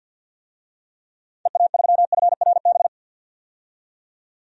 Morse code, heard as a keyed tone, about 51 wpm, 700 Hz. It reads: EA3FRB
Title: DXCC-100 Call Signs - 6 Characters - 50wpm